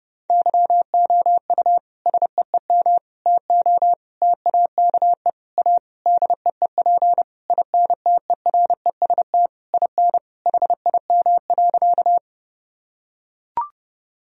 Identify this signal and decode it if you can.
Morse code, heard as a keyed tone, about 32 wpm, 700 Hz. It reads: YOU SEEM TO TAKE A DEEP INTEREST IN HIM.
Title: Sentences from Top 1000 Words - Review Part 3 - 30wpm